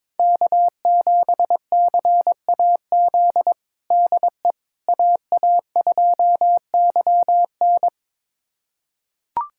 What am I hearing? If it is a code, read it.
Morse code, heard as a keyed tone, about 22 wpm, 700 Hz. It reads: K7CAZ DE AA2YN
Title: USA Callsign DE Callsign - 5 Characters - 22wpm